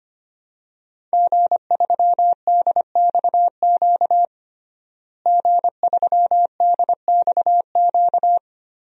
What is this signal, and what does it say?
Morse code, heard as a keyed tone, about 25 wpm, 700 Hz. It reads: G3DXQ G3DXQ
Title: DXCC-100 Call Signs - 5 Characters - Repeated 3x - 25wpm